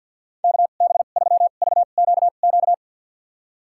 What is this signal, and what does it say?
Morse code, heard as a keyed tone, about 50 wpm, spaced as for 36 wpm, 700 Hz. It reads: KB3VQQ
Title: USA Call Signs - 2x Character Spacing - 50wpm